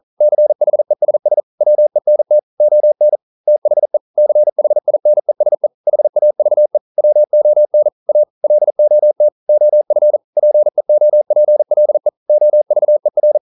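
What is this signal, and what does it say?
Morse code, heard as a keyed tone, about 40 wpm, 600 Hz. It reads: CHESS WENT ON THE CHINESE HAVE WON A LOT OF PEOPLE OVER
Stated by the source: Morse code practice at 41 wpm, from CW med Gnister.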